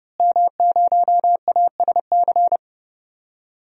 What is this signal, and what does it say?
Morse code, heard as a keyed tone, about 30 wpm, 700 Hz. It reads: M0ASC
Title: DXCC-100 Call Signs - 5 Characters - Repeated 2x - 30wpm